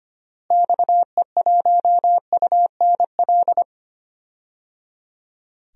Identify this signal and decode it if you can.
Morse code, heard as a keyed tone, about 25 wpm, 700 Hz. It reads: XE1UNL
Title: DXCC-100 Call Signs - 6 Characters - Repeated 3x - 25wpm